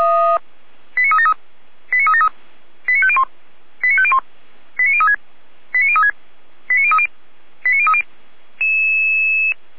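Five-tone selective call (ZVEI), heard as quick runs of five tones, tones of about 80 ms. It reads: 89373 89373 89501 89501 89036 89036 89030 89030
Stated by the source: Received on SDR; ZVEI five-tone call 89099, channel 455, recorded by Feuerwehr Orlamünde.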